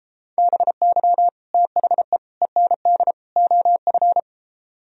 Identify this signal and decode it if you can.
Morse code, heard as a keyed tone, about 33 wpm, 700 Hz. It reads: BY THE END OF